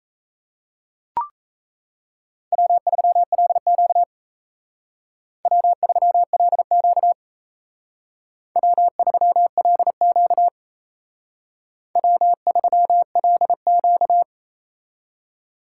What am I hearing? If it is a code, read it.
Morse code, heard as a keyed tone, about 36 wpm, 700 Hz. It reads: W3LQ W3LQ W3LQ W3LQ